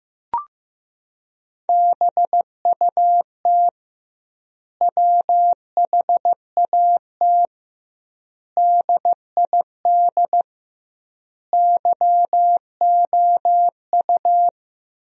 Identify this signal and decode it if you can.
Morse code, heard as a keyed tone, about 15 wpm, 700 Hz. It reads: BUT WHAT DID YOU
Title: 2x Word-Spacing - Sentences from Top 100 Words - 15wpm